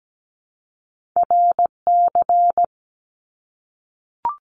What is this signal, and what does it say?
Morse code, heard as a keyed tone, about 17 wpm, 700 Hz. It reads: RC
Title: Top Two Letter Combinations in Words 17wpm